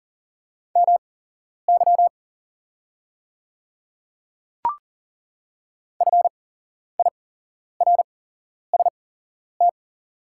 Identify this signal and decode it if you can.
Morse code, heard as a keyed tone, about 38 wpm, spaced as for 11 wpm, 690 Hz. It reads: MY FIRST
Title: 8x Character Spacing - Top 100 Words - 40wpm